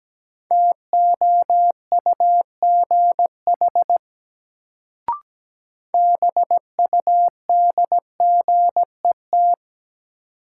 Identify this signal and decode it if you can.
Morse code, heard as a keyed tone, about 17 wpm, 700 Hz. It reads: TOUGH BUDGET
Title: Sentences from Top 1200 Words - New Word Drill - 17wpm